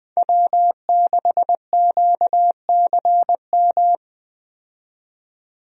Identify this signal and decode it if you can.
Morse code, heard as a keyed tone, about 20 wpm, 700 Hz. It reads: W6QCM
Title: USA Call Signs - 5 Characters - 20wpm